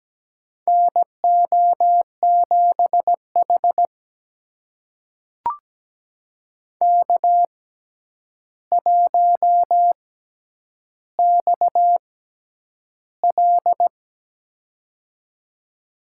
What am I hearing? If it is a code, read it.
Morse code, heard as a keyed tone, about 17 wpm, spaced as for 9 wpm, 700 Hz. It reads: NO7H K1XL
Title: USA Call Signs - 4 Characters - 6x Character Spacing - 17wpm